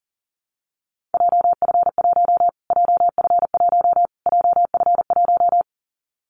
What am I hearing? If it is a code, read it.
Morse code, heard as a keyed tone, about 40 wpm, 700 Hz. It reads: JF1 JF1 JF1
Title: Callsign Prefixes for Top 25 DXCC - Repeated 3x - 40wpm